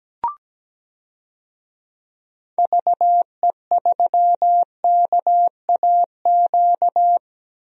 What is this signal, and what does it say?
Morse code, heard as a keyed tone, about 17 wpm, 700 Hz. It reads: VE3KAQ
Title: DXCC-100 Call Signs - 6 Characters - Repeated 2x - 17wpm